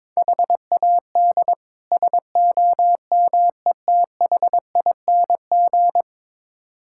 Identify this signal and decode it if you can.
Morse code, heard as a keyed tone, about 22 wpm, 700 Hz. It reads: HAD SOMETHING
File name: practice / 22wpm / Build A Phrase - 5-Word Phrases - 22wpm